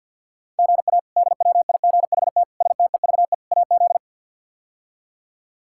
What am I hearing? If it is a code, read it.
Morse code, heard as a keyed tone, about 51 wpm, 700 Hz. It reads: GA DWIGHT STEVE AZ
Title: K1USN SST QSO Practice - 50wpm